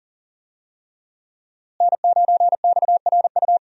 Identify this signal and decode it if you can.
Morse code, heard as a keyed tone, about 40 wpm, 700 Hz. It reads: N9XRU
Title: USA Call Signs - 5 Characters - Repeated 3x - 40wpm